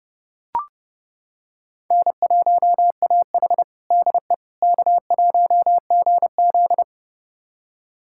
Morse code, heard as a keyed tone, about 30 wpm, 700 Hz. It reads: N1AH DE K1GZ
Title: USA Callsign DE Callsign - 4 Characters - 30wpm